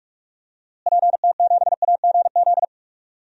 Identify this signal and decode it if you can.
Morse code, heard as a keyed tone, about 46 wpm, 700 Hz. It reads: PT7AGZ
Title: DXCC-100 Call Signs - 6 Characters - Repeated 3x - 45wpm